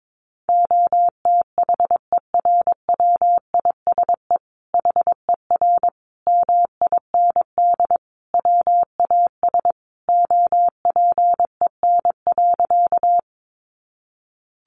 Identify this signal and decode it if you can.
Morse code, heard as a keyed tone, about 22 wpm, 700 Hz. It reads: OTHERWISE HER MIND WAS OPEN.
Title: Sentences from Top 1200 Words 22wpm